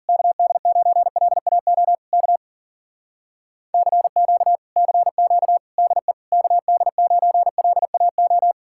Morse code, heard as a keyed tone, about 53 wpm, 700 Hz. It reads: KD9LAO K CQ CQ DE KD9LAO